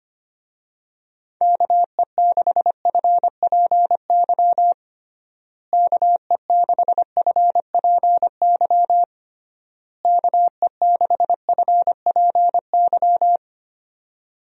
Morse code, heard as a keyed tone, about 25 wpm, 700 Hz. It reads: KE6FPY KE6FPY KE6FPY
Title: USA Call Signs - 6 Characters - Repeated 3x - 25wpm